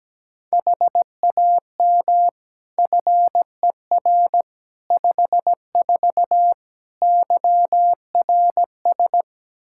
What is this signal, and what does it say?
Morse code, heard as a keyed tone, about 17 wpm, 700 Hz. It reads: HAM FER 54 YRS
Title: Practice QSO - Exchange 3 - More Info - 17wpm